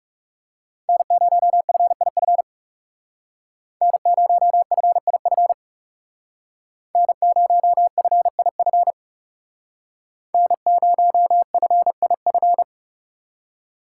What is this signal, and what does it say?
Morse code, heard as a keyed tone, about 36 wpm, 700 Hz. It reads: N0FIF N0FIF N0FIF N0FIF
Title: Speed Racing - USA 1x3 Call Signs 30wpm